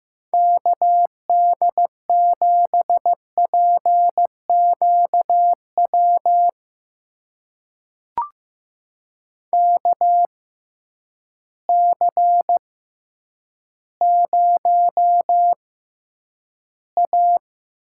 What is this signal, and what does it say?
Morse code, heard as a keyed tone, about 15 wpm, spaced as for 9 wpm, 700 Hz. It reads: KD7PQW KC0A
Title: USA Call Signs - 6 Characters - 6x Character Spacing - 15wpm